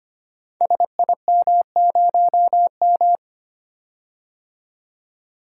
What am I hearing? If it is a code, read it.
Morse code, heard as a keyed tone, about 25 wpm, 700 Hz. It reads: SIM0M
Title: Code Groups - 5 Letter:Numbers - 25wpm